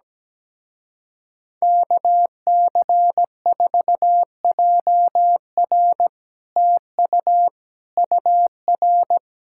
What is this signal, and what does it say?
Morse code, heard as a keyed tone, about 17 wpm, 700 Hz. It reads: KC4JR TU UR
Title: Speed-Racing Common QSO Phrases - 17wpm